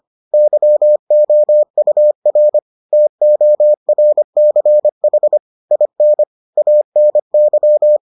Morse code, heard as a keyed tone, about 25 wpm, 600 Hz. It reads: YOUR TORCH IN ANY